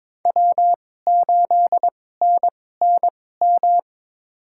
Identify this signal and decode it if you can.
Morse code, heard as a keyed tone, about 22 wpm, spaced as for 16 wpm, 700 Hz. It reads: W8NNM
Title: USA Call Signs - 5 Characters - 2x Character Spacing - 22wpm